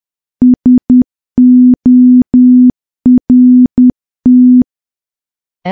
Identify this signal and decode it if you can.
Morse code, heard as a keyed tone, about 10 wpm, 250 Hz. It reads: SORT
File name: morse / morse-training-10WPM